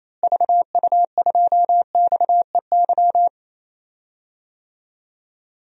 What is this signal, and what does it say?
Morse code, heard as a keyed tone, about 28 wpm, 700 Hz. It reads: VU2XEY
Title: DXCC-100 Call Signs - 6 Characters - Repeated 2x - 28wpm